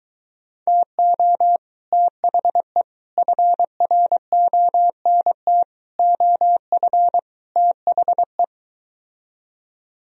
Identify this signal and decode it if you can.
Morse code, heard as a keyed tone, about 23 wpm, 690 Hz. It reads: TO THE FRONT OF THE